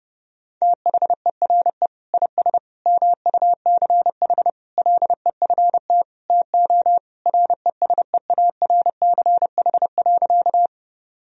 Morse code, heard as a keyed tone, about 32 wpm, 700 Hz. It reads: THERE IS MUCH LEFT TO RESEARCH.
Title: Sentences from Top 1000 Words - Review Part 4 - 30wpm